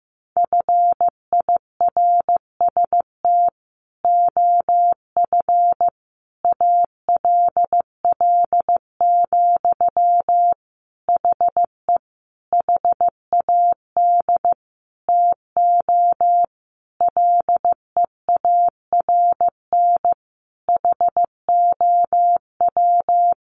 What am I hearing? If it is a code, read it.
Morse code, heard as a keyed tone, about 15 wpm, 700 Hz. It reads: FIRST OF ALL, HE HAD TO LEARN HOW
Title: Sentences from Top 1200 Words 15wpm